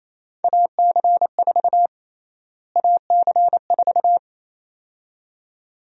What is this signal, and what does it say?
Morse code, heard as a keyed tone, about 28 wpm, 700 Hz. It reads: AC4 AC4
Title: USA Call Sign Prefixes - Repeated 2x - 28wpm